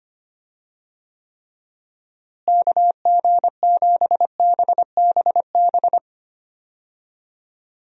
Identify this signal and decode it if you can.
Morse code, heard as a keyed tone, about 25 wpm, 700 Hz. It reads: KG7BBB
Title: USA Call Signs - 6 Characters - Repeated 2x - 25wpm